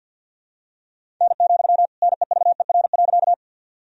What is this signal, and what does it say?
Morse code, heard as a keyed tone, about 46 wpm, 700 Hz. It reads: N, NEVER.
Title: Sentences from Top 300 Words and Abbreviations 50wpm